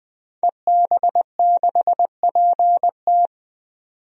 Morse code, heard as a keyed tone, about 21 wpm, 700 Hz. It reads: EB6PT
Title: DXCC-100 Call Signs - 5 Characters - 20wpm